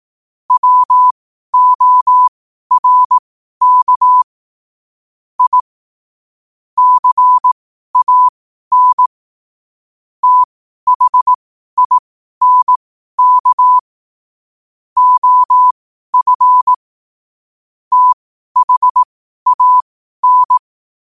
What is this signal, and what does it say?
Morse code, heard as a keyed tone, about 18 wpm, spaced as for 12 wpm, 1000 Hz. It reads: WORK I CAN THINK OF THAN
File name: qotdfiles / 12wpm